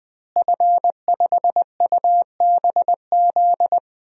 Morse code, heard as a keyed tone, about 21 wpm, 700 Hz. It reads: F5UBZ
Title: DXCC-100 Call Signs - 5 Characters - Repeated 3x - 20wpm